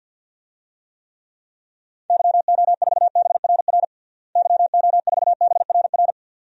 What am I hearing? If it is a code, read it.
Morse code, heard as a keyed tone, about 48 wpm, 700 Hz. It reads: YO4BRR YO4BRR
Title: DXCC-100 Call Signs - 6 Characters - Repeated 2x - 50wpm